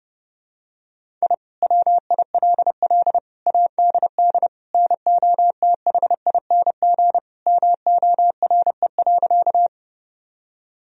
Morse code, heard as a keyed tone, about 30 wpm, 700 Hz. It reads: I WILL ADD NOTHING MORE.
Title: Sentences from Top 1000 Words - Review Part 1 - 30wpm